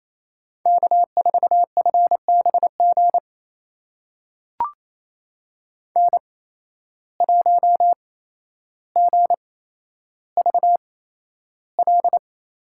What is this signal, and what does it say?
Morse code, heard as a keyed tone, about 28 wpm, spaced as for 12 wpm, 700 Hz. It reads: K4FBG N1GVL
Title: USA Call Signs - 5 Characters - 8x Character Spacing - 28wpm